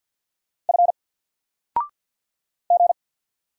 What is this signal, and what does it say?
Morse code, heard as a keyed tone, about 50 wpm, 700 Hz. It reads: F G
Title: Single Letter-Number 50wpm